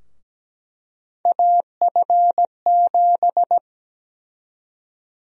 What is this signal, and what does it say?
Morse code, heard as a keyed tone, about 17 wpm, 700 Hz. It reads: AF7
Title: USA Call Sign Prefixes - Repeated 2x - 17wpm